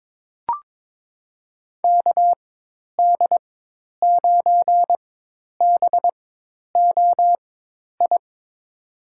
Morse code, heard as a keyed tone, about 22 wpm, spaced as for 10 wpm, 700 Hz. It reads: KD9BOI
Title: USA Call Signs - 4x Character Spacing - 22wpm